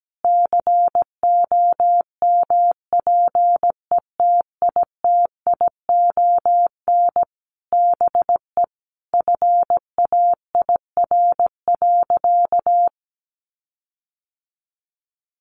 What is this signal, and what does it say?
Morse code, heard as a keyed tone, about 17 wpm, 700 Hz. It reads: COMPETITION BE FAIR.
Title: Sentences from Top 1300 Words 17wpm